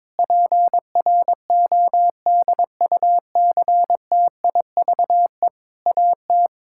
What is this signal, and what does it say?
Morse code, heard as a keyed tone, about 22 wpm, 700 Hz. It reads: PRODUCTIVE AT
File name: Difficult Sentences with Hints - 22wpm